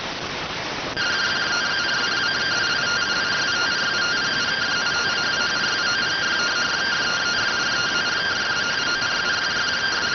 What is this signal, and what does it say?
Two-tone teleprinter signal, about 170 Hz shift.